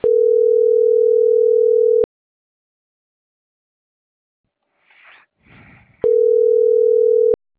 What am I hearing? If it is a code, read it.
Ringback tone (North America), 440 + 480 Hz.